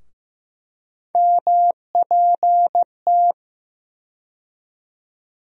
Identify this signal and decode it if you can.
Morse code, heard as a keyed tone, about 15 wpm, 700 Hz. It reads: MPT
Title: USA Call Sign Postfixes - 3 Letters 15wpm